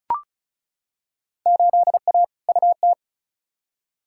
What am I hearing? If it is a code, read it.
Morse code, heard as a keyed tone, about 34 wpm, 700 Hz. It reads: 8A UT
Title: Field Day Exchanges Encore - 35wpm